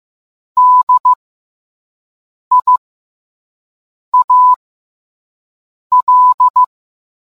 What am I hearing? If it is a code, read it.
Morse code, heard as a keyed tone, about 15 wpm, spaced as for 5 wpm, 1000 Hz. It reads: DIAL